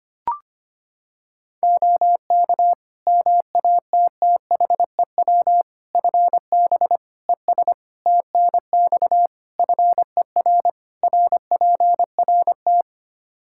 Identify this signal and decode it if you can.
Morse code, heard as a keyed tone, about 25 wpm, 700 Hz. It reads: OK MATTHEW FB ES TNX FER RPRT
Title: Practice QSO - Exchange 2 - Info - 25wpm